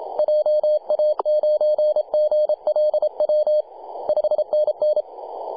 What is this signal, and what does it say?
Morse code, heard as a keyed tone, about 30 wpm, 610 Hz. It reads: JA9GLW 5NN
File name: posts / 2015-july-26-0904z-vk2ssi-18-074mhz-cw